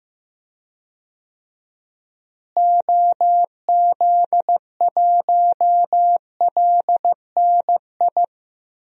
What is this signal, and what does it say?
Morse code, heard as a keyed tone, about 15 wpm, 700 Hz. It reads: OZ1LNI
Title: DXCC-100 Call Signs - 6 Characters - Repeated 3x - 15wpm